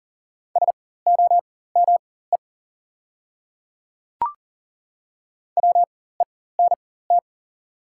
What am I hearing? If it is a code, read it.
Morse code, heard as a keyed tone, about 38 wpm, spaced as for 18 wpm, 700 Hz. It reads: SOME WENT
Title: 4x Character Spacing - Top 200 Words - 40wpm